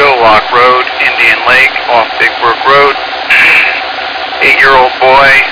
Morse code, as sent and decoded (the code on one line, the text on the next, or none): none